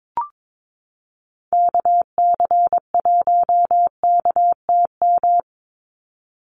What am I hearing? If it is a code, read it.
Morse code, heard as a keyed tone, about 22 wpm, 700 Hz. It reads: KC1KTM
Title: USA Call Signs 22wpm